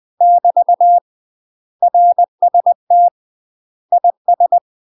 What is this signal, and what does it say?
Morse code, heard as a keyed tone, about 20 wpm, 700 Hz. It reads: <BT> RST IS
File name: UK MASTER QSOs TRAINING / 20WPM / UK MASTER QSOs 0014